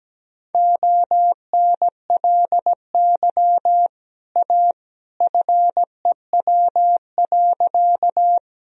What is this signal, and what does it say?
Morse code, heard as a keyed tone, about 17 wpm, 700 Hz. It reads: ONLY A FEW.
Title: Sentences from Top 1000 Words - Review Part 1 - 17wpm